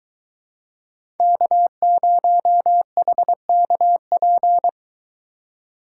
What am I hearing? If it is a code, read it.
Morse code, heard as a keyed tone, about 22 wpm, 700 Hz. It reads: K0HKP